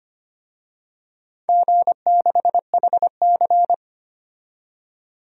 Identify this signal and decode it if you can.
Morse code, heard as a keyed tone, about 25 wpm, 700 Hz. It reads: G6HC
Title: DXCC-100 Call Signs - 4 Characters - 25wpm